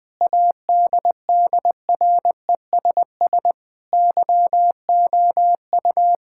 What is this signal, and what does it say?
Morse code, heard as a keyed tone, about 20 wpm, 700 Hz. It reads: ADDRESS YOU
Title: Sentences from Top 1000 Words - Review Part 6 - 20wpm